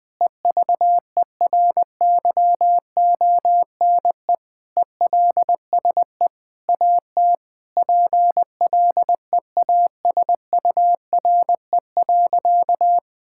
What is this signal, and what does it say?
Morse code, heard as a keyed tone, about 20 wpm, 700 Hz. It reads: EVERYONE ELSE AT PLEASURE.
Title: Sentences from Top 1600 Words 20wpm